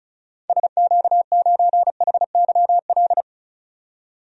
Morse code, heard as a keyed tone, about 34 wpm, 700 Hz. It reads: SQ9HYL
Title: DXCC-100 Call Signs - 6 Characters - 35wpm